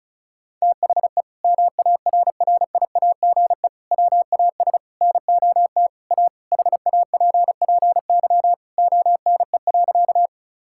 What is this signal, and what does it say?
Morse code, heard as a keyed tone, about 34 wpm, 700 Hz. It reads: THE MARRIAGE WAS NOT A HAPPY ONE.
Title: Sentences from Top 1000 Words - Review Part 5 - 35wpm